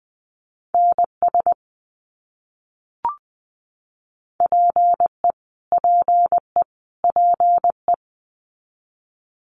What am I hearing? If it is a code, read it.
Morse code, heard as a keyed tone, about 20 wpm, 700 Hz. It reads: NS PE PE PE
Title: Canadian Provinces and Territories - Repeated 3x - 20wpm